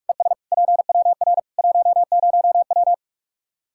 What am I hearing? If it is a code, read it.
Morse code, heard as a keyed tone, about 46 wpm, 700 Hz. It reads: ES PWR 10W
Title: Practice QSO - Exchange 2 - Info - 45wpm